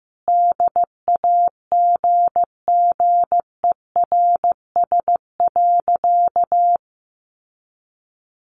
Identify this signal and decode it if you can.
Morse code, heard as a keyed tone, about 15 wpm, 700 Hz. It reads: DAGGERS.
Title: A Sound Of Thunder 15wpm